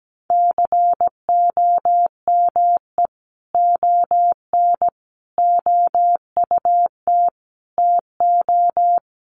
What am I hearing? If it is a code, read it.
Morse code, heard as a keyed tone, about 17 wpm, 700 Hz. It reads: COME ON OUT TO
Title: Sentences from Top 1500 Words 17wpm